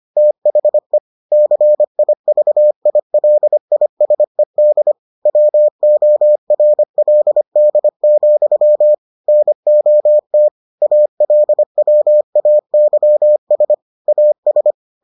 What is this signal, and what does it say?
Morse code, heard as a keyed tone, about 25 wpm, 600 Hz. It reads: THE CIVILISED WORLD, NOT ALWAYS AS